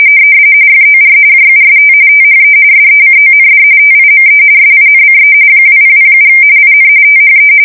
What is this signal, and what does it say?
RTTY, about 45.45 baud. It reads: FOR JUST THE SECOND TIME SINCE 2007 IN THE
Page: 170 Shift 2125/2295